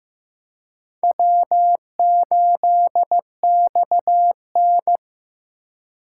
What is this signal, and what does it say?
Morse code, heard as a keyed tone, about 15 wpm, 700 Hz. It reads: W8XN
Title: USA Call Signs - 4 Characters - 4x Character Spacing - 15wpm